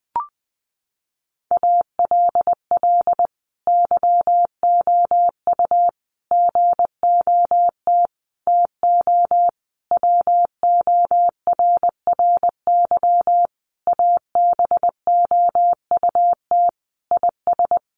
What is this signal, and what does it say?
Morse code, heard as a keyed tone, about 20 wpm, 700 Hz. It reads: ALL YOU GOT TO WORRY ABOUT IS
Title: A Sound Of Thunder 20wpm